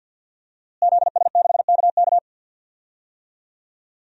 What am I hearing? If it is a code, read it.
Morse code, heard as a keyed tone, about 52 wpm, 700 Hz. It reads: ZS6KK
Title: DXCC-100 Call Signs - 5 Characters - Repeated 3x - 50wpm